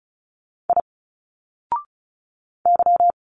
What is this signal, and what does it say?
Morse code, heard as a keyed tone, about 34 wpm, 700 Hz. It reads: I Y
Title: Single Letters 35wpm